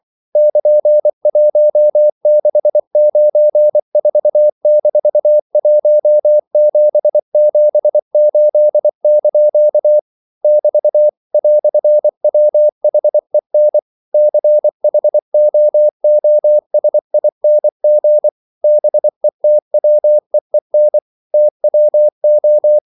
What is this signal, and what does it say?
Morse code, heard as a keyed tone, about 24 wpm, 600 Hz. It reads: <KN>1694-1778) <BT> "WHEN CHOOSING BETWEEN TWO
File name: CW / Practice / EnglishQuotations0324